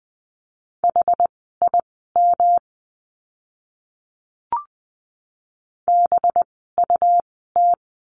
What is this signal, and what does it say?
Morse code, heard as a keyed tone, about 20 wpm, spaced as for 15 wpm, 700 Hz. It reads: HIM BUT
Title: 206 Lesson 33f - Letter j - Words 20wpm